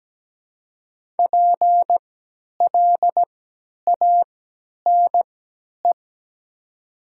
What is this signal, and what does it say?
Morse code, heard as a keyed tone, about 17 wpm, spaced as for 10 wpm, 700 Hz. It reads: PLANE
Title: 3x Character Spacing - Top 500 Words - 17wpm